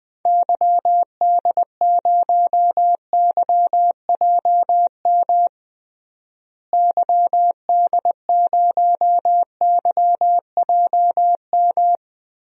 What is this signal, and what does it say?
Morse code, heard as a keyed tone, about 20 wpm, 700 Hz. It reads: YD0YJM YD0YJM
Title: DXCC-100 Call Signs - 6 Characters - Repeated 3x - 20wpm